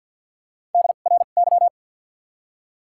Morse code, heard as a keyed tone, about 53 wpm, 700 Hz. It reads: D R Y
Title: 2x Character Spacing - Top 500 Words - 50wpm